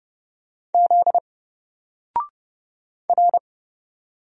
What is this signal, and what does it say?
Morse code, heard as a keyed tone, about 30 wpm, 700 Hz. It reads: Z R
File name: Single-Letter-Number-30wpm